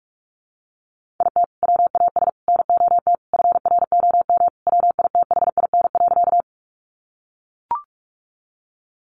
Morse code, heard as a keyed tone, about 44 wpm, 700 Hz. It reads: IT WAS NOT FROM WITHIN.
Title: Sentences from Top 700 Words 45wpm